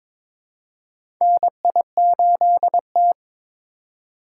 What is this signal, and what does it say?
Morse code, heard as a keyed tone, about 22 wpm, 700 Hz. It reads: NI8T
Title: USA Call Signs - 4 Characters - 3x Character Spacing - 22wpm